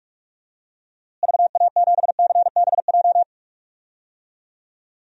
Morse code, heard as a keyed tone, about 43 wpm, 700 Hz. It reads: VA7CBJ
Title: DXCC-100 Call Signs - 6 Characters - Repeated 3x - 45wpm